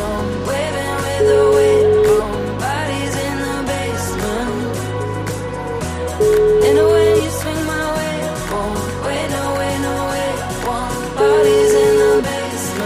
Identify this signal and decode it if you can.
Ringback tone (Europe, 425 Hz), with noise.